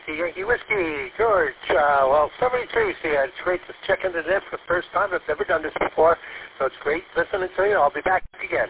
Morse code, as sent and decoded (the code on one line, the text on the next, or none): none